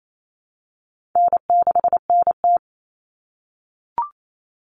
Morse code, heard as a keyed tone, about 28 wpm, 700 Hz. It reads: N6NT
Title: Common Cut Numbers - 4-Digit 28wpm